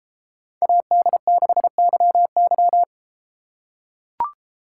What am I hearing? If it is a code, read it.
Morse code, heard as a keyed tone, about 31 wpm, 700 Hz. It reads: AD6YY